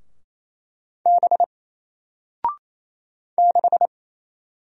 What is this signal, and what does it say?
Morse code, heard as a keyed tone, about 27 wpm, 700 Hz. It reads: B 6
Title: Single Character (Letter:Number:Punctuation:Prosign) 28wpm